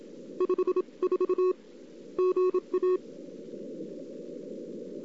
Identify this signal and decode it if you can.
Morse code, heard as a keyed tone, about 27 wpm, spaced as for 24 wpm, 370 Hz. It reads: 54 GA
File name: CW / 2008-ss-cw-k5tr-op-n5rz -02:00